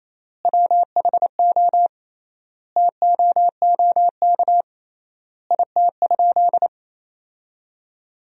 Morse code, heard as a keyed tone, about 28 wpm, 700 Hz. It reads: WHO TOOK IT?
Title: 3x Word-Spacing - Sentences from Top 300 Words - 28wpm